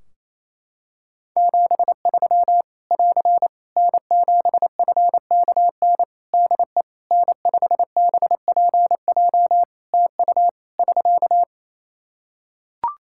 Morse code, heard as a keyed tone, about 28 wpm, 700 Hz. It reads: 73 <AR> N7FKN DE N5BPJ TU <SK>
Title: Practice QSO - Exchange 4 - Ending - 28wpm